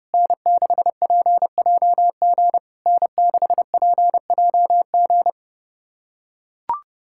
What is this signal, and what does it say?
Morse code, heard as a keyed tone, about 30 wpm, 690 Hz. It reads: N6PJG N6PJG
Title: SOTA QSO Practice 30wpm